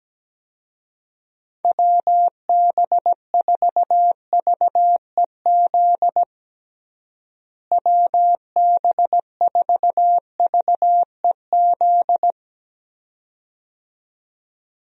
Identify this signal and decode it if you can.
Morse code, heard as a keyed tone, about 17 wpm, 700 Hz. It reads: WB4VEZ WB4VEZ
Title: USA Call Signs - 6 Characters - Repeated 2x - 17wpm